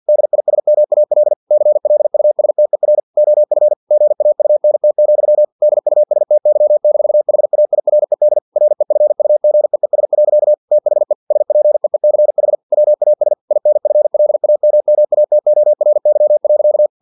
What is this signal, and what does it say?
Morse code, heard as a keyed tone, about 50 wpm, 600 Hz. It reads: DISMAL CLUSTER OF GAUNT, DUSTY-HAIRED REFUGEES. THE SPEECH WAS INFLAMMATORY.
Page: Never 49wpm